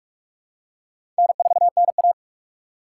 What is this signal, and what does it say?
Morse code, heard as a keyed tone, about 44 wpm, 690 Hz. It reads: N4NA
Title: USA Call Signs - 4 Characters - 2x Character Spacing - 45wpm